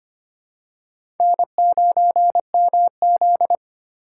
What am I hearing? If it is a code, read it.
Morse code, heard as a keyed tone, about 25 wpm, 700 Hz. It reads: N9MZ
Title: USA Call Signs - 3x Character Spacing - 25wpm